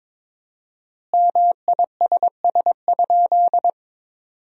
Morse code, heard as a keyed tone, about 22 wpm, 700 Hz. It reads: MISS?
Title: 4x Word-Spacing - Sentences from Top 500 Words - 22wpm